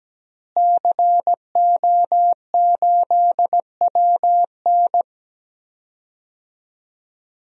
Morse code, heard as a keyed tone, about 17 wpm, 700 Hz. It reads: CO8WN
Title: DXCC-100 Call Signs - 5 Characters - Repeated 2x - 17wpm